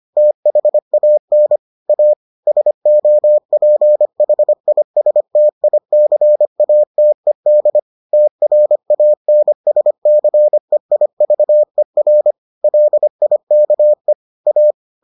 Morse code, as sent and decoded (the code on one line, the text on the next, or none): - .... .- -. / .- / ... --- .--. .... .. ... - .. -.-. .- - . -.. / - .-. .- -. ... -.-. .. ...- . .-. / .-.. .. -.- . / .-
THAN A SOPHISTICATED TRANSCEIVER LIKE A